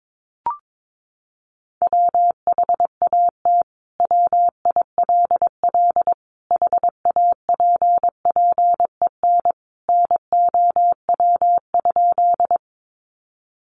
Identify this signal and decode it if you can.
Morse code, heard as a keyed tone, about 22 wpm, 700 Hz. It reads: WHAT WILL HAPPEN NOW?
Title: Sentences from Top 400 Words 22wpm